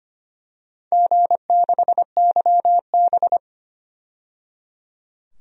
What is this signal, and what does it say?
Morse code, heard as a keyed tone, about 25 wpm, 700 Hz. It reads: G6YB
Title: DXCC-100 Call Signs - 4 Characters - 25wpm